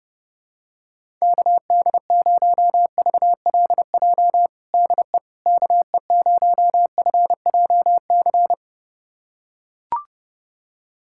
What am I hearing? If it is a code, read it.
Morse code, heard as a keyed tone, about 30 wpm, 700 Hz. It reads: KD0VLJ DE KE0FJC
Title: USA Callsign DE Callsign - 6 Characters - 30wpm